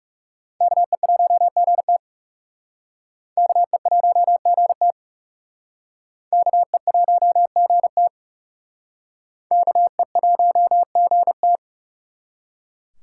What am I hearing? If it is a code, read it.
Morse code, heard as a keyed tone, about 37 wpm, 700 Hz. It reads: KE1GT KE1GT KE1GT KE1GT
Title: Speed Racing - USA 2x2 Call Signs 30wpm